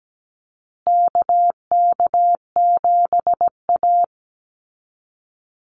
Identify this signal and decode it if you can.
Morse code, heard as a keyed tone, about 17 wpm, 700 Hz. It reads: KK7A
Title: Call Signs Repeated 2x - 17wpm